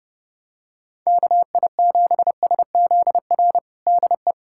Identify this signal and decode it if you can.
Morse code, heard as a keyed tone, about 29 wpm, 700 Hz. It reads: KI7SZR DE
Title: USA Callsign DE Callsign - 6 Characters - 30wpm